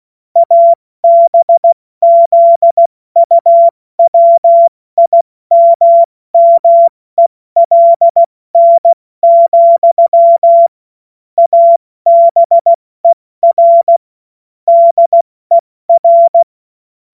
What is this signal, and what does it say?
Morse code, heard as a keyed tone, about 16 wpm, 680 Hz. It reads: ABZUWIMMELN, ABER DER
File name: Morsekurs / Witze / Witz 09 Januar 2024 70 F80